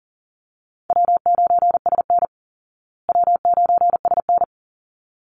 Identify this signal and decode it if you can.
Morse code, heard as a keyed tone, about 40 wpm, 700 Hz. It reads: W9SN W9SN
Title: Call Signs Repeated 3x - 40wpm